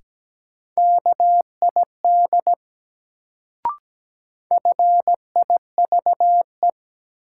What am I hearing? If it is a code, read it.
Morse code, heard as a keyed tone, about 17 wpm, 700 Hz. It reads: KID FIVE
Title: Sets of 5 Words - New Word Drill - 17wpm